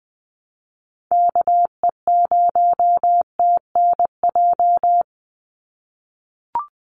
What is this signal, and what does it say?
Morse code, heard as a keyed tone, about 20 wpm, 700 Hz. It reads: KE0TNJ
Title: USA Call Signs 20wpm